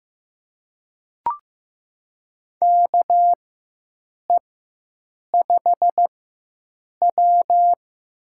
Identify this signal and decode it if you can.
Morse code, heard as a keyed tone, about 15 wpm, spaced as for 7 wpm, 700 Hz. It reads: KE5W
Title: USA Call Signs - 4x Character Spacing - 15wpm